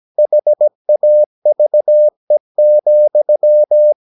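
Morse code, heard as a keyed tone, about 17 wpm, 600 Hz. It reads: HAVE,
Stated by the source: Slow (QRS) Morse practice at 17 wpm, from CW med Gnister.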